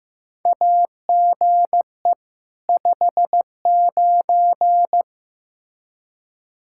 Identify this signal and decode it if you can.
Morse code, heard as a keyed tone, about 15 wpm, 700 Hz. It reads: AGE 59
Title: K1USN SST QSO Practice 15wpm